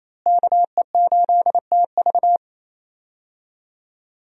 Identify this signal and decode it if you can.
Morse code, heard as a keyed tone, about 28 wpm, 700 Hz. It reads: KE8TV
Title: USA Call Signs - 5 Characters - Repeated 2x - 28wpm